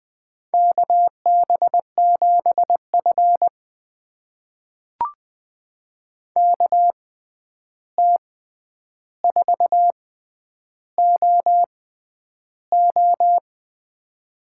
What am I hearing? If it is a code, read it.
Morse code, heard as a keyed tone, about 20 wpm, spaced as for 10 wpm, 700 Hz. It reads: KB7F KT4OO
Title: USA Call Signs - 6x Character Spacing - 20wpm